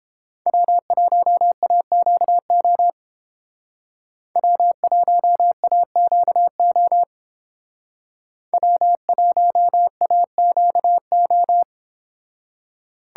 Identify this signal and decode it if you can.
Morse code, heard as a keyed tone, about 28 wpm, 700 Hz. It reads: W1AQO W1AQO W1AQO